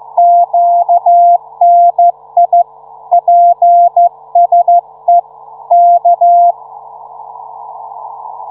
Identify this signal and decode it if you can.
Morse code, heard as a keyed tone, about 13 wpm, 700 Hz. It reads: QNI PSE K